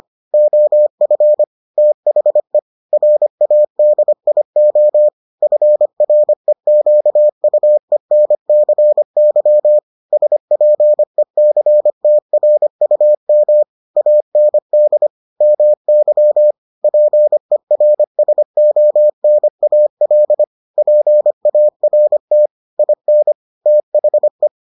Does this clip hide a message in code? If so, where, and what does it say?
Yes: Morse code from 0.3 to 24.5 s, heard as a keyed tone, about 25 wpm, 600 Hz. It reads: OF THE RADIO FREQUENCY SPECTRUM AND MY PERSONAL PART IN THE